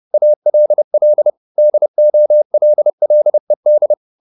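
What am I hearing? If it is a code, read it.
Morse code, heard as a keyed tone, about 30 wpm, 600 Hz. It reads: ALL DOLLED
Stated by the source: Morse code at 30 wpm, the novel Never by Ken Follett in English, from CW med Gnister.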